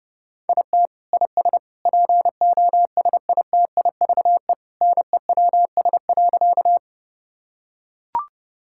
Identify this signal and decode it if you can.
Morse code, heard as a keyed tone, about 30 wpm, 700 Hz. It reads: IT IS POSITIVE NEWS.
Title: Sentences from Top 1000 Words - Review Part 5 - 30wpm